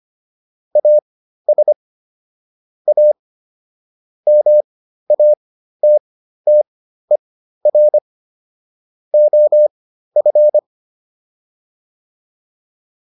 Morse code, heard as a keyed tone, about 25 wpm, spaced as for 13 wpm, 600 Hz. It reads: AS A MATTER OF
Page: /intermediate-practice-files/